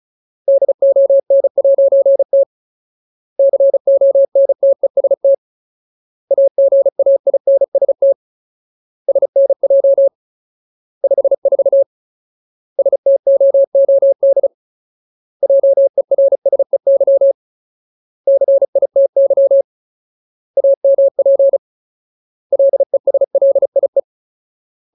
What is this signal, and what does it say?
Morse code, heard as a keyed tone, about 34 wpm, 550 Hz. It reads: DON'T CONTEST AGAINST SNJ 54 STOOD JERSEY CITY AMP LESLIE
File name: Ultimate Word List / 35WPM Ultimate